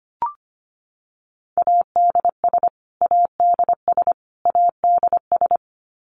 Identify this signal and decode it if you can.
Morse code, heard as a keyed tone, about 25 wpm, 700 Hz. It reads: ADS ADS ADS
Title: Abbreviations Repeated 3x - 25wpm